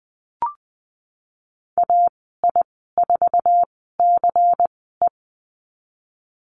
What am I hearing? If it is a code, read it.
Morse code, heard as a keyed tone, about 20 wpm, spaced as for 15 wpm, 700 Hz. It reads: AI4CE
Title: 122 Lesson 20g - Number 3 - Call Signs 20wpm